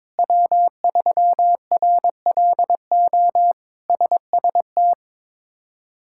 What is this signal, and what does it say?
Morse code, heard as a keyed tone, about 22 wpm, 700 Hz. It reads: W3RLO SST
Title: K1USN SST QSO Practice - 22wpm